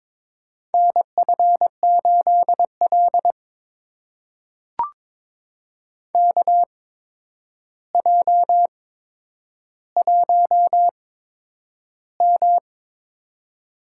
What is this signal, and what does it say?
Morse code, heard as a keyed tone, about 22 wpm, spaced as for 9 wpm, 700 Hz. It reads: NF8L KJ1M
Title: USA Call Signs - 4 Characters - 8x Character Spacing - 22wpm